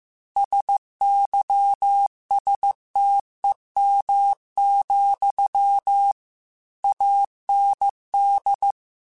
Morse code, heard as a keyed tone, about 15 wpm, 800 Hz. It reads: SYSTEM, AND
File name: Technical Subjects / Technical Subjects-mp3s / Sw Pwr Sup @13